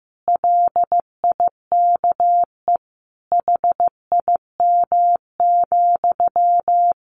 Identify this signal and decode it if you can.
Morse code, heard as a keyed tone, about 15 wpm, 700 Hz. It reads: LIKE HIM,
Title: Sentences from Top 800 Words 15wpm